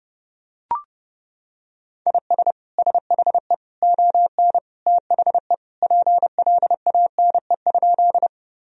Morse code, heard as a keyed tone, about 30 wpm, 700 Hz. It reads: IS SHE ON THE PLANE?
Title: Sentences from Top 1000 Words - Review Part 3 - 30wpm